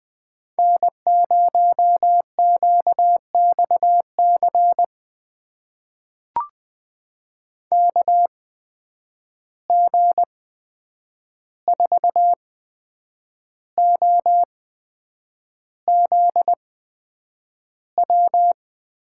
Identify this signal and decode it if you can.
Morse code, heard as a keyed tone, about 20 wpm, spaced as for 8 wpm, 700 Hz. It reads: N0QXC KG4OZW
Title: USA Call Signs - 8x Character Spacing - 20wpm